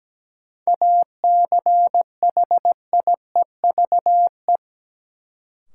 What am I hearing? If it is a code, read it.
Morse code, heard as a keyed tone, about 17 wpm, 700 Hz. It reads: ACHIEVE
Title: Sentences from Top 1000 Words - Review Part 5 - New Word Drill - 17wpm